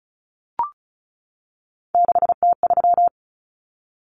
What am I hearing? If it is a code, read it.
Morse code, heard as a keyed tone, about 34 wpm, 700 Hz. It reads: 6T3
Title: Common Cut Numbers - 3-Digit 35wpm